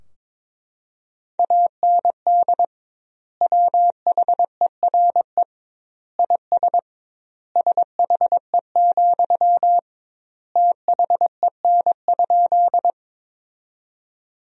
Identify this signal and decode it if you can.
Morse code, heard as a keyed tone, about 22 wpm, 700 Hz. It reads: AND WHERE IS SHE, THEN?
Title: 2x Word-Spacing - Sentences from Top 200 Words - 22wpm